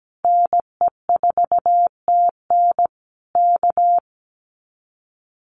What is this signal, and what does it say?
Morse code, heard as a keyed tone, about 17 wpm, 700 Hz. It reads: NE4TN K
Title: Common QSO Phrases Encore 17wpm